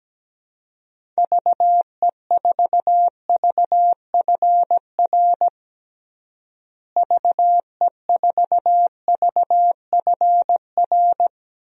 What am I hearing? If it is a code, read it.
Morse code, heard as a keyed tone, about 17 wpm, 700 Hz. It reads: VE4VFR VE4VFR
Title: DXCC-100 Call Signs - 6 Characters - Repeated 2x - 17wpm